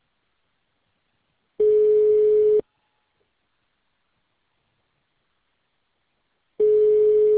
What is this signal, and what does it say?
Ringback tone (Europe, 425 Hz).